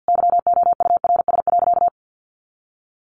Morse code, heard as a keyed tone, about 50 wpm, 700 Hz. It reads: YOURS.
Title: A Sound Of Thunder 50wpm